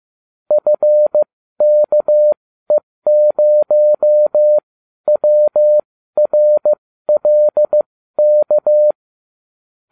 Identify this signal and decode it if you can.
Morse code, heard as a keyed tone, about 15 wpm, spaced as for 12 wpm, 600 Hz. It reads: FKE0WRLK